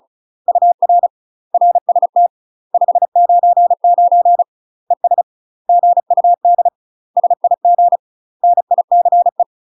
Morse code, heard as a keyed tone, about 36 wpm, 700 Hz. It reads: UR RST 599 ES GUD SIG NICE
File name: UK MASTER QSOs TRAINING / 35WPM / UK MASTER QSOs 0017